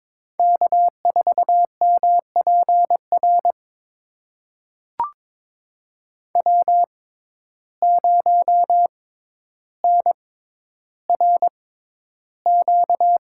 Morse code, heard as a keyed tone, about 22 wpm, spaced as for 11 wpm, 700 Hz. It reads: K4MPR W0NRQ
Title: USA Call Signs - 5 Characters - 6x Character Spacing - 22wpm